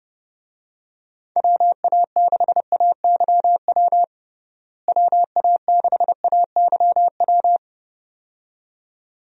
Morse code, heard as a keyed tone, about 30 wpm, 700 Hz. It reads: WA6AYW WA6AYW
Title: USA Call Signs - 6 Characters - Repeated 2x - 30wpm